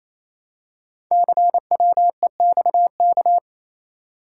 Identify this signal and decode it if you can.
Morse code, heard as a keyed tone, about 27 wpm, 700 Hz. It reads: CWEXK
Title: Code Groups - 5 Letters - 28wpm